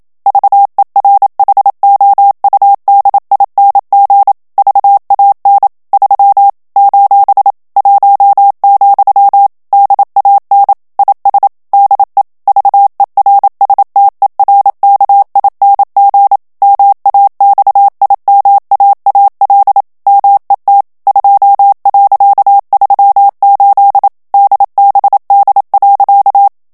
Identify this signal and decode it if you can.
Morse code, heard as a keyed tone, about 27 wpm, 800 Hz. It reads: VERHOUDING VAN 3 : 1, DAN IS DE VERSTERKING MAXIMAAL MET 2.38 DBD.
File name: HOLLANDAIS texte clair 1 vitesse 26 mots minute